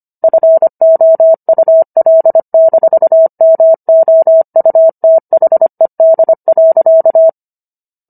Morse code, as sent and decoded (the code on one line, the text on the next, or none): ..-. --- ..- .-.. -....- -- --- ..- - .... -.. .-.-.-
FOUL-MOUTHED.